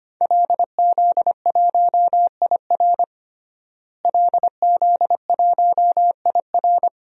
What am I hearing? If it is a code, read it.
Morse code, heard as a keyed tone, about 25 wpm, 700 Hz. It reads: LZ1IR LZ1IR
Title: DXCC-100 Call Signs - 5 Characters - Repeated 3x - 25wpm